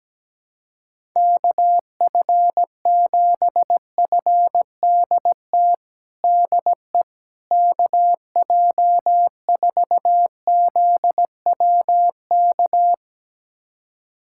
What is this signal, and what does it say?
Morse code, heard as a keyed tone, about 17 wpm, 700 Hz. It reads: KF7FDT DE KJ4ZWK
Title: USA Callsign DE Callsign - 6 Characters - 17wpm